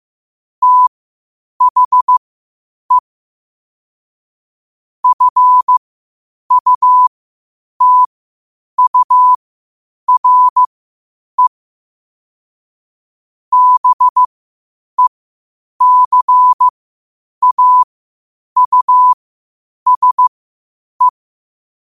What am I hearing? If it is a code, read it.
Morse code, heard as a keyed tone, about 15 wpm, spaced as for 8 wpm, 1000 Hz. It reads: THE FUTURE BECAUSE
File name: qotdfiles / 8wpm